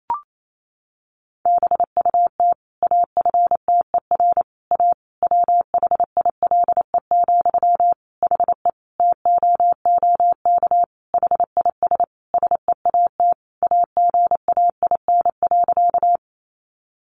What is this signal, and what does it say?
Morse code, heard as a keyed tone, about 28 wpm, 700 Hz. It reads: BUT AFTER A WHILE, HE TOOK HIS SEAT AGAIN.
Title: Sentences from Top 1500 Words 28wpm